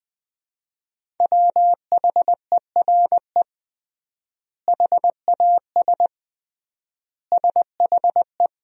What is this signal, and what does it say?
Morse code, heard as a keyed tone, about 20 wpm, 700 Hz. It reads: WHERE HAS SHE
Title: 3x Word-Spacing - Sentences from Top 300 Words - 20wpm